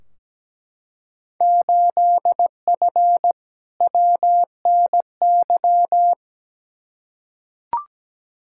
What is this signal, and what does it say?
Morse code, heard as a keyed tone, about 17 wpm, 700 Hz. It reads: 8F WNY
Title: Field Day Exchanges Encore - 17wpm